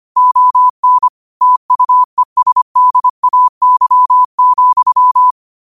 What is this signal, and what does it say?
Morse code, heard as a keyed tone, about 25 wpm, 1000 Hz. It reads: ON TUESDAY,
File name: qotdfiles / 25wpm